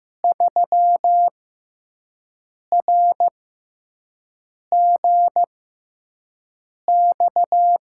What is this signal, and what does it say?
Morse code, heard as a keyed tone, about 15 wpm, spaced as for 5 wpm, 700 Hz. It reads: 3RGX
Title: USA Call Signs - 6 Characters - 6x Character Spacing - 15wpm